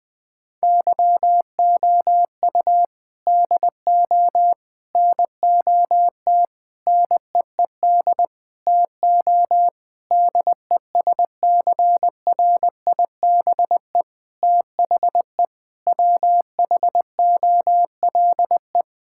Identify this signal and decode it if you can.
Morse code, heard as a keyed tone, about 20 wpm, 700 Hz. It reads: YOU DO NOT NEED TO DESCRIBE THE WHOLE
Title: Sentences from Top 1600 Words 20wpm